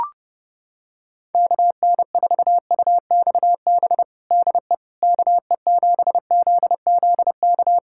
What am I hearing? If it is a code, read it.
Morse code, heard as a keyed tone, about 30 wpm, 700 Hz. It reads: KN4UXB DE KE7ZZK
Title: USA Callsign DE Callsign - 6 Characters - 30wpm